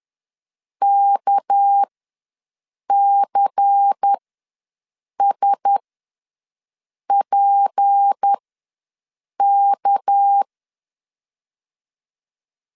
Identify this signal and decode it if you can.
Morse code, heard as a keyed tone, about 10 wpm, spaced as for 6 wpm, 800 Hz. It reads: KCSPK
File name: LEZIONI / 6 PKX0 / 6 PK ws 10wpm 3745